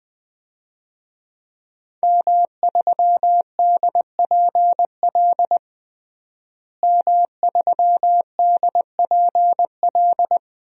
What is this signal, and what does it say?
Morse code, heard as a keyed tone, about 20 wpm, 700 Hz. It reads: M3DPL M3DPL
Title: DXCC-100 Call Signs - 5 Characters - Repeated 2x - 20wpm